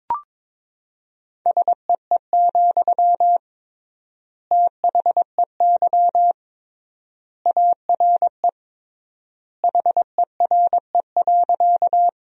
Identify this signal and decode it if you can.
Morse code, heard as a keyed tone, about 22 wpm, 700 Hz. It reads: SEE, THEY ARE HERE.
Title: 3x Word-Spacing - Sentences from Top 200 Words - 22wpm